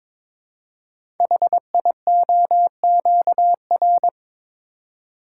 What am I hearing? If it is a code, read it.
Morse code, heard as a keyed tone, about 22 wpm, 700 Hz. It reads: HIOQR
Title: Code Groups - 5 Letter:Numbers - 22wpm